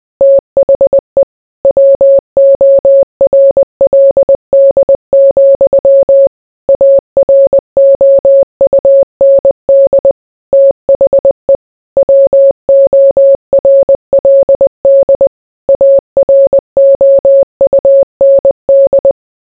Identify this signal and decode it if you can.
Morse code, heard as a keyed tone, about 20 wpm, 550 Hz. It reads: THE WORLD, AROUND THE WORLD AROUND